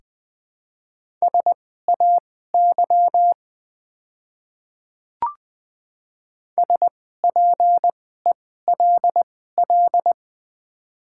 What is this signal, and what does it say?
Morse code, heard as a keyed tone, about 20 wpm, spaced as for 14 wpm, 690 Hz. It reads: SAY SPELL
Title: 2x Character Spacing - Top 300 Words - 20wpm